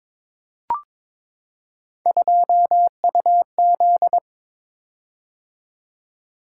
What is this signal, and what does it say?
Morse code, heard as a keyed tone, about 22 wpm, 700 Hz. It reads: 2UZ
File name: practice / 22wpm / Three Letter-Number Combinations - 22wpm